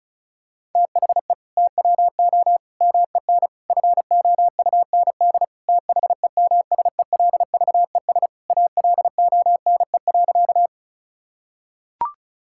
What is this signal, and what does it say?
Morse code, heard as a keyed tone, about 36 wpm, 700 Hz. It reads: THE TWO MEN FOUND THEMSELVES ALONE.
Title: Sentences from Top 1000 Words - Review Part 5 - 35wpm